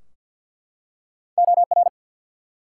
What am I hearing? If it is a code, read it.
Morse code, heard as a keyed tone, about 50 wpm, 700 Hz. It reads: OR
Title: Two Letter Words - 50wpm